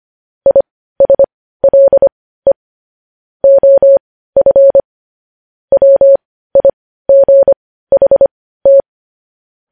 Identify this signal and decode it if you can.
Morse code, heard as a keyed tone, about 25 wpm, spaced as for 15 wpm, 550 Hz. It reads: ISLE OF WIGHT
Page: news headlines in morse code at 15 wpm